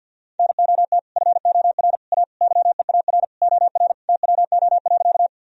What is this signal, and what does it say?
Morse code, heard as a keyed tone, about 50 wpm, 700 Hz. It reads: NOT FOR A YEAR OR TWO.
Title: Sentences from Top 1000 Words - Review Part 1 - 50wpm